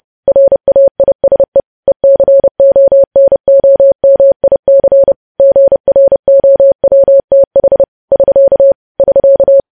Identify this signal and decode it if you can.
Morse code, heard as a keyed tone, about 30 wpm, 550 Hz. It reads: RAISE ECONOMIC GROWTH <SK> <SK>
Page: news-headlines-in-morse-code-at-30-wpm